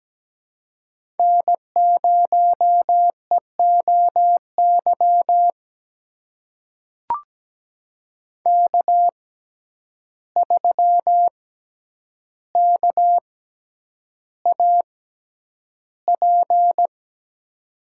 Morse code, heard as a keyed tone, about 17 wpm, spaced as for 9 wpm, 700 Hz. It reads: N0EOY K3KAP
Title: USA Call Signs - 5 Characters - 6x Character Spacing - 17wpm